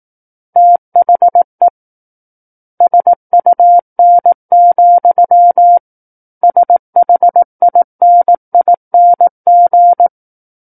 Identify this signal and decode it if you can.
Morse code, heard as a keyed tone, about 18 wpm, 700 Hz. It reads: THE SUN, SHINING
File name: 18 WPM